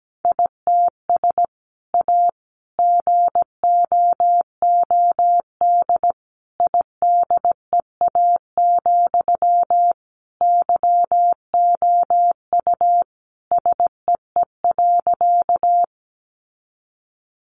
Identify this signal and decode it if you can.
Morse code, heard as a keyed tone, about 17 wpm, 700 Hz. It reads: ITS A GOOD IDEA, YOU SEE.
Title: Sentences from Top 600 Words 17wpm